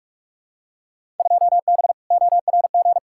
Morse code, heard as a keyed tone, about 42 wpm, 700 Hz. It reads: 2B ORG
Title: Field Day Exchanges Encore - 45wpm